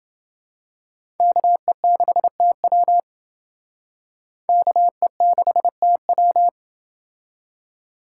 Morse code, heard as a keyed tone, about 27 wpm, 700 Hz. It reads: KE6TW KE6TW